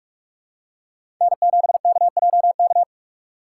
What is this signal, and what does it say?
Morse code, heard as a keyed tone, about 42 wpm, 700 Hz. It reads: N7KJK
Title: USA Call Signs - 5 Characters - 3x Character Spacing - 45wpm